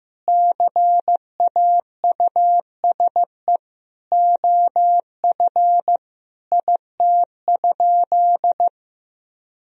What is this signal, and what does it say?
Morse code, heard as a keyed tone, about 15 wpm, 700 Hz. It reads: CAUSE OF IT?
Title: Sentences from Top 1000 Words - Review Part 3 - 15wpm